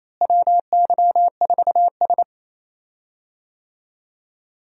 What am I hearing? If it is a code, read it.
Morse code, heard as a keyed tone, about 27 wpm, 700 Hz. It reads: WY4S
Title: USA Call Signs - 4 Characters - Repeated 2x - 28wpm